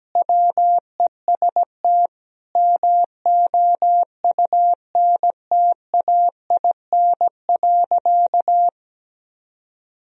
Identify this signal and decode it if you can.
Morse code, heard as a keyed tone, about 17 wpm, 700 Hz. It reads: WEST MOUNTAIN.
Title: Sentences from Top 1000 Words - Review Part 2 - 17wpm